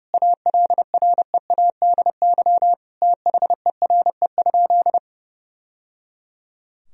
Morse code, heard as a keyed tone, about 30 wpm, 700 Hz. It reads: ALREADY THERE?
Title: Sentences from Top 1000 Words - Review Part 4 - 30wpm